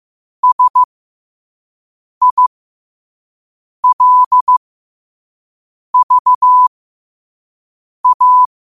Morse code, heard as a keyed tone, about 15 wpm, spaced as for 5 wpm, 1000 Hz. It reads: SILVA
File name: qotdfiles / 5wpm